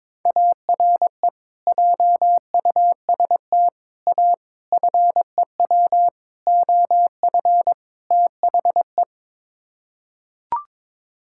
Morse code, heard as a keyed tone, about 22 wpm, 700 Hz. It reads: ARE JUST A FEW OF THE
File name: practice / 22wpm / Sets of 6 Words - 22wpm